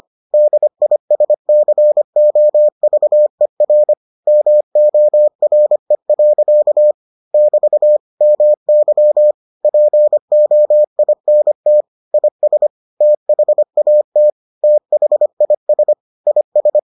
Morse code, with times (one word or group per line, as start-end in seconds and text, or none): 0.3-3.9 DISCOVER
4.3-6.9 MORE.
7.3-8.0 <BT>
8.2-9.3 MY
9.6-11.8 POINT
12.1-12.7 IS
13.0-14.3 THAT
14.6-15.9 THIS
16.3-16.8 IS